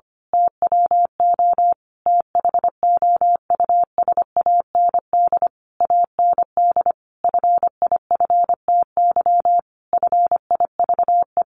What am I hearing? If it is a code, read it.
Morse code, heard as a keyed tone, about 25 wpm, 700 Hz. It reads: TWO THOUSAND AND FIFTY FIVE
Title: A Sound Of Thunder 25wpm